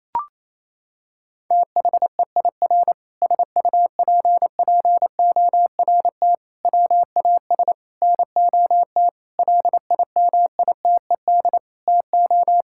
Morse code, heard as a keyed tone, about 28 wpm, 700 Hz. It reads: THEIR SUPPORT WAS NOT LIMITED TO
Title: Sentences from Top 1000 Words - Review Part 6 - 28wpm